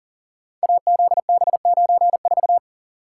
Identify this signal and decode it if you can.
Morse code, heard as a keyed tone, about 40 wpm, 700 Hz. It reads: AZB94
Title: Code Groups - 5 Letter:Numbers - 40wpm